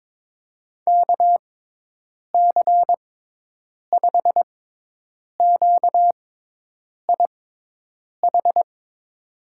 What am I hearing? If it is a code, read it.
Morse code, heard as a keyed tone, about 22 wpm, spaced as for 8 wpm, 700 Hz. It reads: KC5QIH
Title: USA Call Signs - 6 Characters - 6x Character Spacing - 22wpm